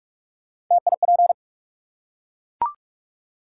Morse code, heard as a keyed tone, about 44 wpm, 700 Hz. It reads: TIP
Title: Three Letter Words - 45wpm